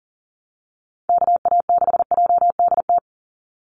Morse code, heard as a keyed tone, about 40 wpm, 700 Hz. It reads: KA6JDT
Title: USA Call Signs 40wpm